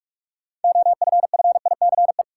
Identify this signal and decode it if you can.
Morse code, heard as a keyed tone, about 44 wpm, 700 Hz. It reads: OFFICE
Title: Sentences from Top 1000 Words - New Word Drill - 45wpm